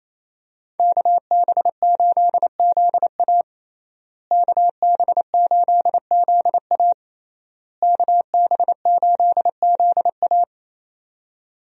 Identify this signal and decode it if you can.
Morse code, heard as a keyed tone, about 28 wpm, 700 Hz. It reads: KB8ZA KB8ZA KB8ZA
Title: USA Call Signs - 5 Characters - Repeated 3x - 28wpm